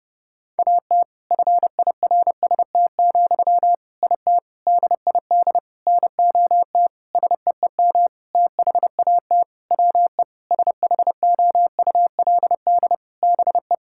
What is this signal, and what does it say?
Morse code, heard as a keyed tone, about 30 wpm, 700 Hz. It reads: AT FIRST, IT DID NOT SEEM THAT WE SHOULD BE
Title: Sentences from Top 1000 Words - Review Part 6 - 30wpm